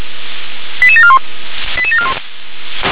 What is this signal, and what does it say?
Five-tone selective call (ZVEI), heard as quick runs of five tones, tones of about 70 ms. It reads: 88741 88741